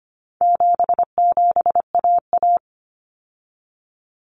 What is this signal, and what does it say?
Morse code, heard as a keyed tone, about 25 wpm, 700 Hz. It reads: 77AA
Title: Common Cut Numbers - 4-Digit 25wpm